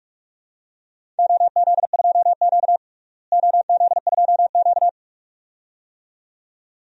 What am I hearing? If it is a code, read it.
Morse code, heard as a keyed tone, about 41 wpm, 700 Hz. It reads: OZ2Q OZ2Q
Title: DXCC-100 Call Signs - 4 Characters - Repeated 2x - 45wpm